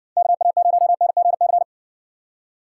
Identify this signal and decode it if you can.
Morse code, heard as a keyed tone, about 67 wpm, 700 Hz. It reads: KA0NGC